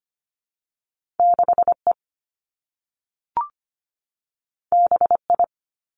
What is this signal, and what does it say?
Morse code, heard as a keyed tone, about 25 wpm, 700 Hz. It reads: 6E BI
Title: Two Challenging Letter-Number Combinations 25wpm